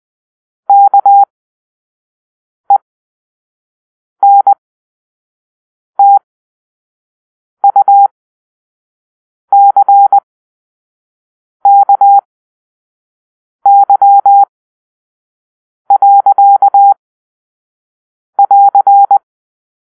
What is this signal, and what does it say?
Morse code, heard as a keyed tone, about 20 wpm, spaced as for 5 wpm, 800 Hz. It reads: KENTUCKY.<AR>
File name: arrl / Grp-5 Drill-4